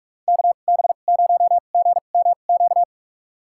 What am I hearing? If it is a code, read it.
Morse code, heard as a keyed tone, about 43 wpm, spaced as for 32 wpm, 700 Hz. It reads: KB0GMQ
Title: USA Call Signs - 2x Character Spacing - 45wpm